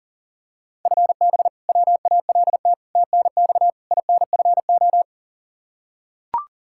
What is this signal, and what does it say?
Morse code, heard as a keyed tone, about 40 wpm, 700 Hz. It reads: FB WALT TNX INFO
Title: Practice QSO - Exchange 3 - More Info - 40wpm